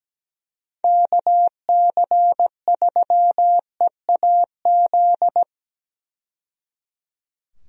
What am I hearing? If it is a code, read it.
Morse code, heard as a keyed tone, about 17 wpm, 700 Hz. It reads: KC3EAZ
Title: USA Call Signs - 6 Characters - 17wpm